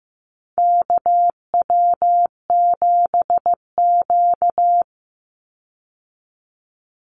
Morse code, heard as a keyed tone, about 15 wpm, 700 Hz. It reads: KW7Q
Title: Call Signs Repeated 3x - 15wpm